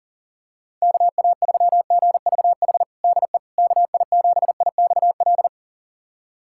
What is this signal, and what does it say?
Morse code, heard as a keyed tone, about 41 wpm, 690 Hz. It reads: KA3GVH DE KI7IXL
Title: USA Callsign DE Callsign - 6 Characters - 40wpm